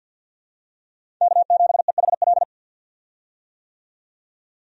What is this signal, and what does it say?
Morse code, heard as a keyed tone, about 47 wpm, 700 Hz. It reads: K7EHL
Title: USA Call Signs - 5 Characters - 50wpm